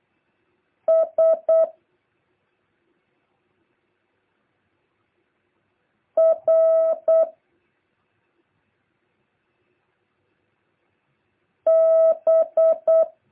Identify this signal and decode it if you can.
Morse code, heard as a keyed tone, about 8 wpm, 650 Hz. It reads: S R B
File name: jh8cbh / 2023cw